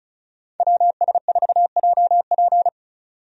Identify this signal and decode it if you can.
Morse code, heard as a keyed tone, about 36 wpm, 700 Hz. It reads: WS4JP
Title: USA Call Signs - 5 Characters - 35wpm